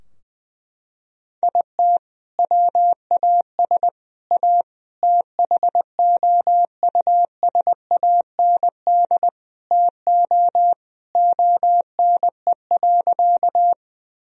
Morse code, heard as a keyed tone, about 20 wpm, 700 Hz. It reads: IT WAS A THOUSAND TO ONE.
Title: Sentences from Top 1000 Words - Review Part 3 - 20wpm